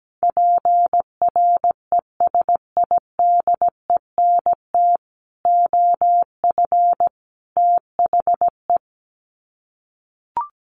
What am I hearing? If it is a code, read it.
Morse code, heard as a keyed tone, about 17 wpm, 700 Hz. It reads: PRESIDENT OF THE
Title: Sets of 3 Words 17wpm